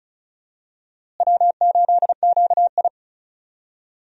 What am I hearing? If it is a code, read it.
Morse code, heard as a keyed tone, about 34 wpm, 700 Hz. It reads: W8QI
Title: USA Call Signs - 4 Characters - Repeated 3x - 35wpm